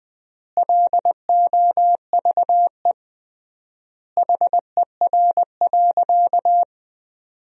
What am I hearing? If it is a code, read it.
Morse code, heard as a keyed tone, about 21 wpm, 700 Hz. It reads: LOVE HER.
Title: 3x Word-Spacing - Sentences from Top 400 Words - 20wpm